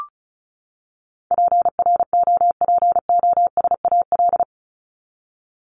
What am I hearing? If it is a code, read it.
Morse code, heard as a keyed tone, about 34 wpm, 700 Hz. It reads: PROPOSAL
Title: Top 1751-2000 Words 35wpm